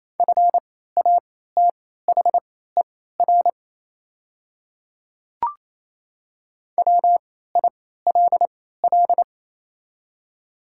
Morse code, heard as a keyed tone, about 27 wpm, spaced as for 16 wpm, 700 Hz. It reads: FATHER WILL
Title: 3x Character Spacing - Top 300 Words - 28wpm